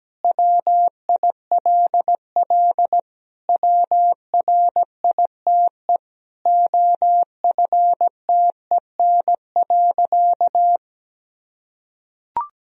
Morse code, heard as a keyed tone, about 17 wpm, 700 Hz. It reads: WILL WRITE OFTEN.
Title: Sentences from Top 1000 Words - Review Part 1 - 17wpm